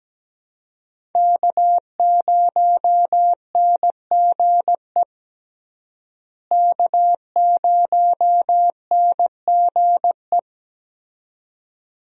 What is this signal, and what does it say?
Morse code, heard as a keyed tone, about 17 wpm, 700 Hz. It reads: K0NGE K0NGE
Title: USA Call Signs - 5 Characters - Repeated 2x - 17wpm